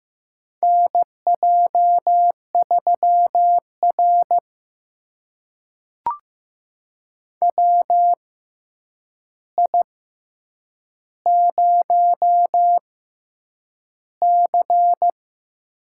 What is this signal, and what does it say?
Morse code, heard as a keyed tone, about 15 wpm, spaced as for 8 wpm, 700 Hz. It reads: NJ3R WI0C
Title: USA Call Signs - 4 Characters - 6x Character Spacing - 15wpm